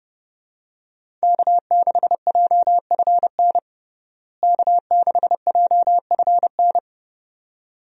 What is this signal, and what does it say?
Morse code, heard as a keyed tone, about 29 wpm, 700 Hz. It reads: K6JFN K6JFN
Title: USA Call Signs - 5 Characters - Repeated 2x - 30wpm